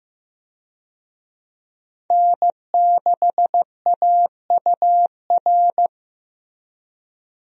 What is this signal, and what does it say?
Morse code, heard as a keyed tone, about 16 wpm, 700 Hz. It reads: N6AUR
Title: USA Call Signs - 5 Characters - Repeated 3x - 15wpm